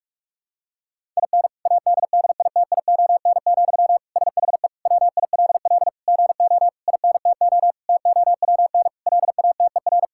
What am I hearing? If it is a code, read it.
Morse code, heard as a keyed tone, about 41 wpm, 700 Hz. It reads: IN ADDITION, SHE WILL GO INTO TOWN LATER
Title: Sentences from Top 1000 Words - Review Part 6 - 45wpm